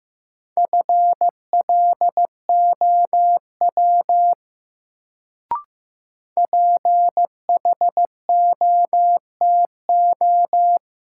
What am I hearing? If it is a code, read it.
Morse code, heard as a keyed tone, about 15 wpm, 700 Hz. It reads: FLOW PHOTO
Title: Sentences from Top 1200 Words - New Word Drill Alt - 15wpm